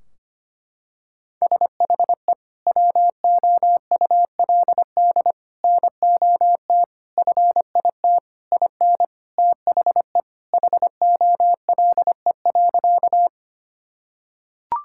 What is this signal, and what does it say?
Morse code, heard as a keyed tone, about 25 wpm, 700 Hz. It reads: SHE WOULD NOT FIT IN THE HOLE.
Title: Sentences from Top 1600 Words 25wpm